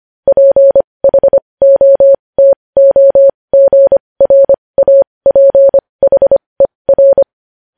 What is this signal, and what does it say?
Morse code, heard as a keyed tone, about 25 wpm, spaced as for 20 wpm, 550 Hz. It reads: PHOTOGRAPHER